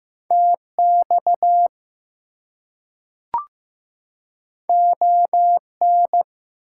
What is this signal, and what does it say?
Morse code, heard as a keyed tone, about 15 wpm, 700 Hz. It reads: TX ON
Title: US States and Canadian Provinces 15wpm